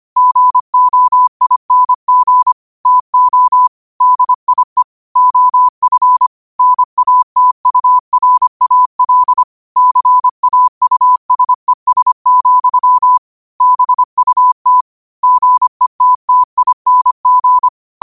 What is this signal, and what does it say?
Morse code, heard as a keyed tone, about 25 wpm, 1000 Hz. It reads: GOING TO DIE OF NATURAL CAUSES, BUT GETTING
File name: 25wpm